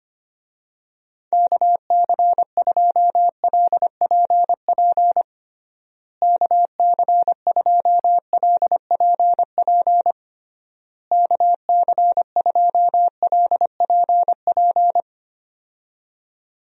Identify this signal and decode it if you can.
Morse code, heard as a keyed tone, about 25 wpm, 700 Hz. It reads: KC2LPP KC2LPP KC2LPP
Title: USA Call Signs - 6 Characters - Repeated 3x - 25wpm